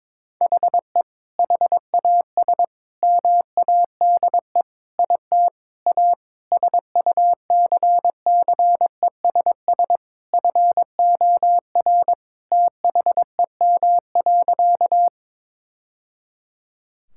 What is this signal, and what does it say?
Morse code, heard as a keyed tone, about 22 wpm, 700 Hz. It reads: HE HAS MADE IT A SUCCESS FOR THEM.
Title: Sentences from Top 1000 Words - Review Part 6 - 22wpm